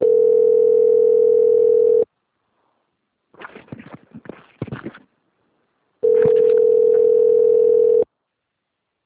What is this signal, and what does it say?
Ringback tone (North America), 440 + 480 Hz.
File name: out- - - - - .7909